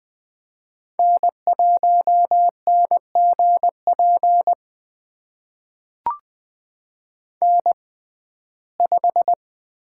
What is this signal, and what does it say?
Morse code, heard as a keyed tone, about 20 wpm, spaced as for 15 wpm, 700 Hz. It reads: N1NGP N5
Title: USA Call Signs - 5 Characters - 6x Character Spacing - 20wpm